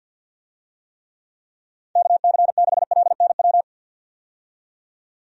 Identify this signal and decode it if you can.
Morse code, heard as a keyed tone, about 51 wpm, 700 Hz. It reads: KC6LNW
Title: USA Call Signs - 6 Characters - 50wpm